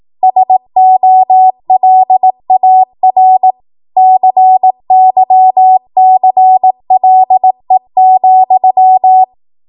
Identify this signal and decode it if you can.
Morse code, heard as a keyed tone, about 18 wpm, 750 Hz. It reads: SOLAR CYCLE,
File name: Archive / 18 WPM / 18WPM